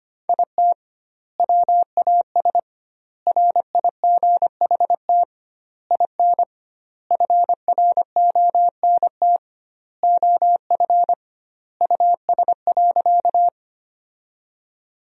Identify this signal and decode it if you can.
Morse code, heard as a keyed tone, about 25 wpm, 700 Hz. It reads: IT WAS RIGHT IN FRONT OF US.
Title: 2x Word-Spacing - Sentences from Top 500 Words - 25wpm